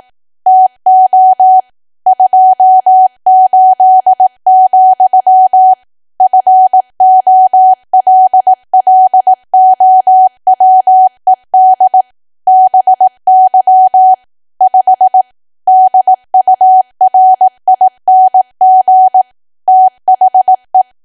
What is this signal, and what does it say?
Morse code, heard as a keyed tone, about 18 wpm, 750 Hz. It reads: TO 28, FOLLOWED BY 5 DURING THE
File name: cwa / W1AW-Level2 / 18WPM